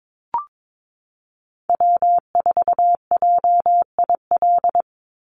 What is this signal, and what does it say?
Morse code, heard as a keyed tone, about 22 wpm, 700 Hz. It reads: W4JIL
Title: USA Call Signs 22wpm